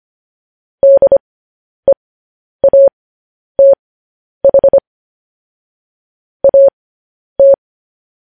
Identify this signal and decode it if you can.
Morse code, heard as a keyed tone, about 25 wpm, spaced as for 10 wpm, 550 Hz. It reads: DEATH AT
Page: news headlines in morse code at 10 wpm